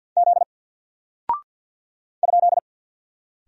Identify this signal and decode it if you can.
Morse code, heard as a keyed tone, about 48 wpm, 700 Hz. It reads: Z ?
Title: Single Character (Letter:Number:Punctuation:Prosign) Alt 50wpm